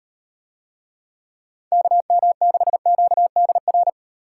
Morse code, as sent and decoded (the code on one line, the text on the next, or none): -.- -- -.... --.- -.. .-.
KM6QDR